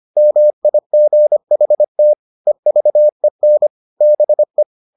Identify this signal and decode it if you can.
Morse code, heard as a keyed tone, about 25 wpm, 600 Hz. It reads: MIGHT EVEN BE